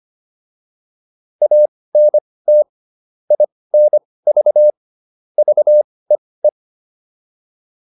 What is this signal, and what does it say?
Morse code, heard as a keyed tone, about 25 wpm, spaced as for 18 wpm, 600 Hz. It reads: ANT INV VEE
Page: intermediate-practice-files